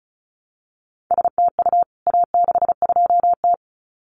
Morse code, heard as a keyed tone, about 34 wpm, 700 Hz. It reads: STU A62T
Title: CWT Exchange 35wpm